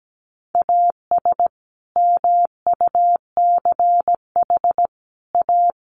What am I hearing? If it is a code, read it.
Morse code, heard as a keyed tone, about 17 wpm, 700 Hz. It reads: AS MUCH A
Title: Sets of 5 Words Encore2 17wpm